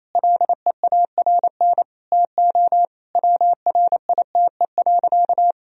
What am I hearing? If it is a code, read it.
Morse code, heard as a keyed tone, about 28 wpm, 700 Hz. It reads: LEARN TO WRITE.
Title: Sentences from Top 1000 Words - Review Part 1 - 28wpm